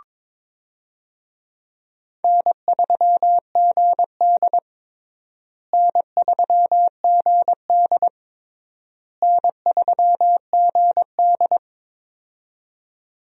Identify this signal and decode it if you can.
Morse code, heard as a keyed tone, about 22 wpm, 700 Hz. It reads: N3GD N3GD N3GD
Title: USA Call Signs - 4 Characters - Repeated 3x - 22wpm